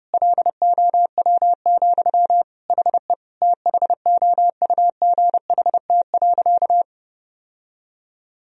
Morse code, heard as a keyed tone, about 30 wpm, 700 Hz. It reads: LOW, HE THOUGHT.
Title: Sentences from Top 1000 Words - Review Part 1 - 30wpm